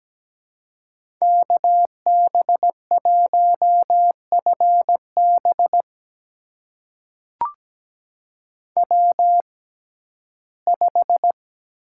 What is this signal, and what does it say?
Morse code, heard as a keyed tone, about 17 wpm, spaced as for 12 wpm, 700 Hz. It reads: KB1FB W5
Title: USA Call Signs - 5 Characters - 6x Character Spacing - 17wpm